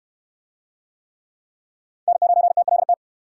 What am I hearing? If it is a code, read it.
Morse code, heard as a keyed tone, about 64 wpm, 700 Hz. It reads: N9TLT